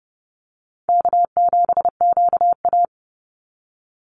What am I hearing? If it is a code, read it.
Morse code, heard as a keyed tone, about 30 wpm, 700 Hz. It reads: K7QA
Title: Easy Call Signs - 30wpm